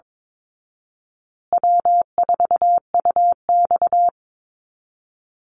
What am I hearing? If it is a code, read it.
Morse code, heard as a keyed tone, about 22 wpm, 700 Hz. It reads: W4UX
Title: Easy Call Signs Repeated 2x - 22wpm